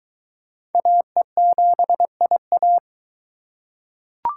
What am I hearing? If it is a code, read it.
Morse code, heard as a keyed tone, about 22 wpm, 700 Hz. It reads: AE7IA